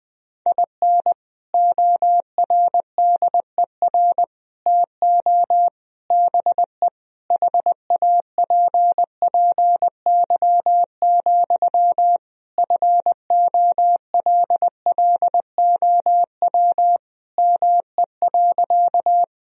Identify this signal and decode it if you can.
Morse code, heard as a keyed tone, about 20 wpm, 700 Hz. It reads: IN ORDER TO BE HAPPY, FOLLOW ME.
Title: Sentences from Top 1000 Words - Review Part 2 - 20wpm